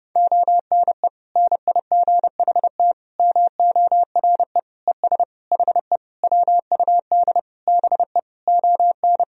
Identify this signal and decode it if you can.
Morse code, heard as a keyed tone, about 30 wpm, 700 Hz. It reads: ONE NIGHT MORE ES HE WUD BE ON
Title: Sentences from Top 300 Words and Abbreviations 30wpm